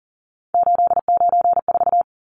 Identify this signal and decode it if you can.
Morse code, heard as a keyed tone, about 40 wpm, 700 Hz. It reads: 894
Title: Numbers - 3-Digit 40wpm